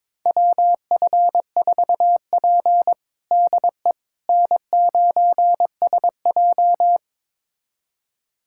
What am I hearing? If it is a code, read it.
Morse code, heard as a keyed tone, about 22 wpm, 700 Hz. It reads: WF4P DE N9SJ
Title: USA Callsign DE Callsign - 4 Characters - 22wpm